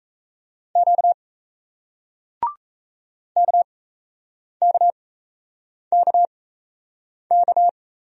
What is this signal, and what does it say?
Morse code, heard as a keyed tone, about 38 wpm, spaced as for 8 wpm, 700 Hz. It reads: Q KKKK